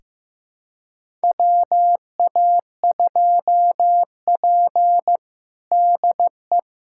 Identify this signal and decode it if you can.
Morse code, heard as a keyed tone, about 15 wpm, 700 Hz. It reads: WA2P DE
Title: USA Callsign DE Callsign - 4 Characters - 15wpm